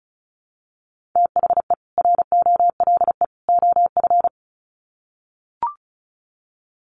Morse code, heard as a keyed tone, about 34 wpm, 700 Hz. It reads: THE ROLE OF
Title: Sets of 3 Words 35wpm